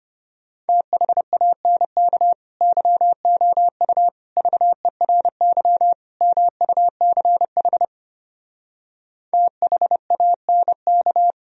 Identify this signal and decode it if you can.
Morse code, heard as a keyed tone, about 33 wpm, 700 Hz. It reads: THANK YOU VERY MUCH THANK